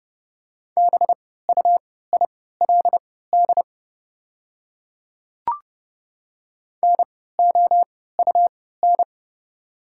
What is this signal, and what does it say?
Morse code, heard as a keyed tone, about 30 wpm, spaced as for 17 wpm, 700 Hz. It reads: BUILD NOUN
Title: 3x Character Spacing - Top 500 Words - 30wpm